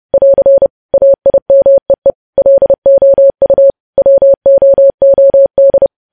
Morse code, heard as a keyed tone, about 30 wpm, 550 Hz. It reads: <AR> AIMEE LOU WOOD
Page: news headlines in morse code at 30 wpm